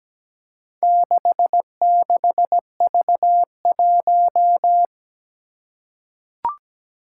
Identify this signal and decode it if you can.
Morse code, heard as a keyed tone, about 17 wpm, 700 Hz. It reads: 66V1
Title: Four Challenging Letter-Number Combinations 17wpm